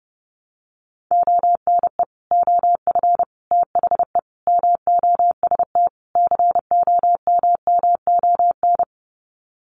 Morse code, heard as a keyed tone, about 29 wpm, 700 Hz. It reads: ONE OF THE MOST COMMON
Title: Sets of 5 Words Encore2 30wpm